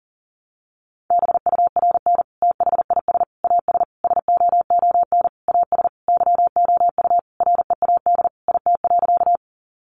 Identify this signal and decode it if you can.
Morse code, heard as a keyed tone, about 40 wpm, 700 Hz. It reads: BURN THIS AS SOON AS YOU READ IT.
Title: Sentences from Top 1300 Words 40wpm